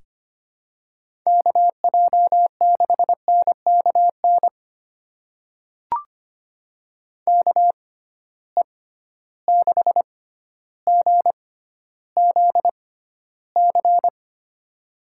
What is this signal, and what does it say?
Morse code, heard as a keyed tone, about 25 wpm, spaced as for 13 wpm, 700 Hz. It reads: KJ6NKN KE6GZC
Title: USA Call Signs - 6 Characters - 6x Character Spacing - 25wpm